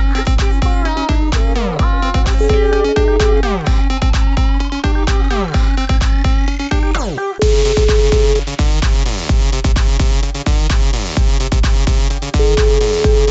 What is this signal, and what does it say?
Ringback tone (Europe, 425 Hz), with noise.